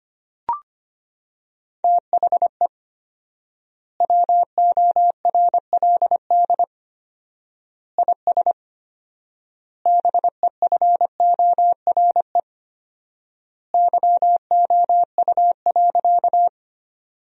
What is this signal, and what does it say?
Morse code, heard as a keyed tone, about 25 wpm, 700 Hz. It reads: THE WORLD IS BEFORE YOU.
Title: 4x Word-Spacing - Sentences from Top 200 Words - 25wpm